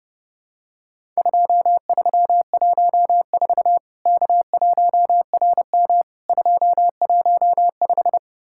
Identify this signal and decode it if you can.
Morse code, heard as a keyed tone, about 32 wpm, 700 Hz. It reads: 2314 K1RM 215
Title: Worked All Europe DX Contest - QTC Exchange - 30wpm